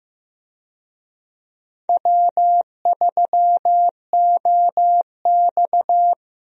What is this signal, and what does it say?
Morse code, heard as a keyed tone, about 15 wpm, 700 Hz. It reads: W3OX
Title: USA Call Signs - 4 Characters - Repeated 2x - 15wpm